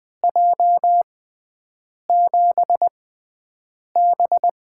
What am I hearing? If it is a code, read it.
Morse code, heard as a keyed tone, about 20 wpm, spaced as for 7 wpm, 700 Hz. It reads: J7B
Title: USA Call Signs - 6 Characters - 6x Character Spacing - 20wpm